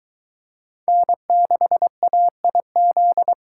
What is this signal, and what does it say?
Morse code, heard as a keyed tone, about 22 wpm, 700 Hz. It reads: N6AIZ